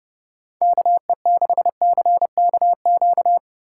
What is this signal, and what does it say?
Morse code, heard as a keyed tone, about 30 wpm, 700 Hz. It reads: KE6CKQ
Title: USA Call Signs - 6 Characters - 3x Character Spacing - 30wpm